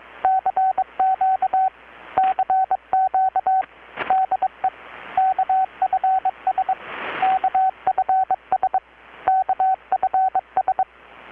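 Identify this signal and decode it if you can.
Morse code, heard as a keyed tone, about 21 wpm, 760 Hz, with radio noise. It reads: CQ CQ DE KFS KFS KFS
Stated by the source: Morse from KPH's Night of Nights XX maritime radiotelegraphy event, received in Chiapas, Mexico about 3,800 km away on a Kenwood R-600 with a grounded 10 m long-wire antenna.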